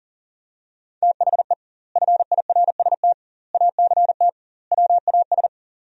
Morse code, heard as a keyed tone, about 40 wpm, 700 Hz. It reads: THE FIRST ACT WAS
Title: 2x Word-Spacing - Sentences from Top 200 Words - 40wpm